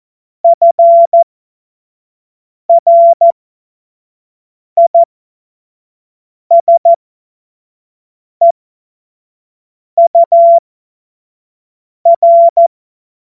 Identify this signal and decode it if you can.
Morse code, heard as a keyed tone, about 14 wpm, spaced as for 5 wpm, 680 Hz. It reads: FRISEUR
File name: Morsekurs / Witze / Witz 18. April 2023 25 F70